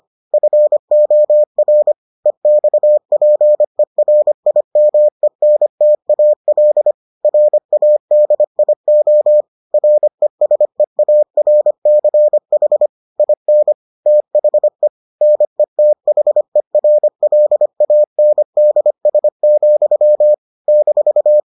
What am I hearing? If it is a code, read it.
Morse code, heard as a keyed tone, about 25 wpm, 600 Hz. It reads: FOR EXPERIMENTAL RADIO RESEARCH IN THE NETHERLANDS, -